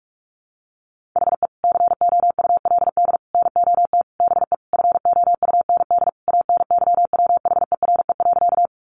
Morse code, heard as a keyed tone, about 44 wpm, 700 Hz. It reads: HE COULD NOT BE FOUND ANYWHERE.
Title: Sentences from Top 1400 Words 45wpm